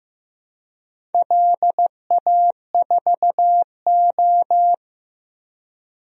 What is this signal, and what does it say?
Morse code, heard as a keyed tone, about 15 wpm, 700 Hz. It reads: LA4O
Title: DXCC-100 Call Signs - 4 Characters - 15wpm